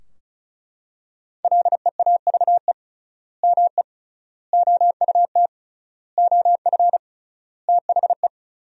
Morse code, heard as a keyed tone, about 36 wpm, 700 Hz. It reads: LEAVE ME OUT OF THE
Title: 3x Word-Spacing - Sentences from Top 400 Words - 35wpm